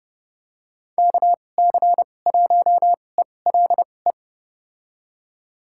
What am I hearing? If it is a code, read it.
Morse code, heard as a keyed tone, about 30 wpm, spaced as for 22 wpm, 700 Hz. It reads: KC1ELE
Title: USA Call Signs - 6 Characters - 2x Character Spacing - 30wpm